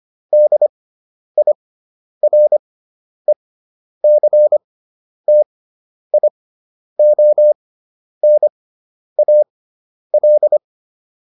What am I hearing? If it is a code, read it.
Morse code, heard as a keyed tone, about 25 wpm, spaced as for 10 wpm, 600 Hz. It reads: DIRECTIONAL